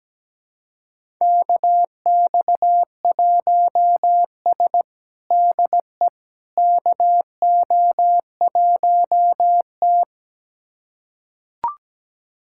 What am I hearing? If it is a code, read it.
Morse code, heard as a keyed tone, about 17 wpm, 700 Hz. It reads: KX1S DE KO1T
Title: USA Callsign DE Callsign - 4 Characters - 17wpm